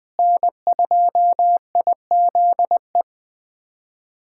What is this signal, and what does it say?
Morse code, heard as a keyed tone, about 20 wpm, 700 Hz. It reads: N2IZE
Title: USA Callsign DE Callsign 20wpm